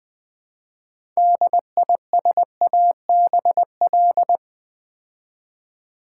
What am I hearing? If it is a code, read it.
Morse code, heard as a keyed tone, about 20 wpm, 700 Hz. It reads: DISABL
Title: Build A Word - 2 Letters At A Time - 8-Letter Words - 20wpm